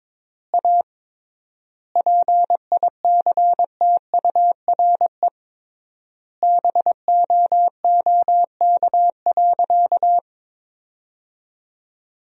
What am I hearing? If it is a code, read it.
Morse code, heard as a keyed tone, about 22 wpm, 700 Hz. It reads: A PICTURE BOOK.
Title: 3x Word-Spacing - Sentences from Top 500 Words - 22wpm